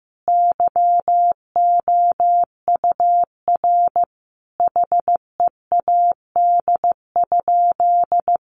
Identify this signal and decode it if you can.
Morse code, heard as a keyed tone, about 15 wpm, 700 Hz. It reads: YOUR HEAD?
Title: Sentences from Top 600 Words 15wpm